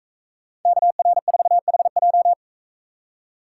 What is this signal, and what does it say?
Morse code, heard as a keyed tone, about 41 wpm, 700 Hz. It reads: KR4HJ